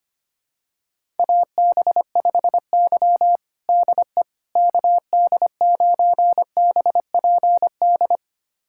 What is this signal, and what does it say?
Morse code, heard as a keyed tone, about 25 wpm, 700 Hz. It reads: AB5Y DE KD9BPD
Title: USA Callsign DE Callsign 25wpm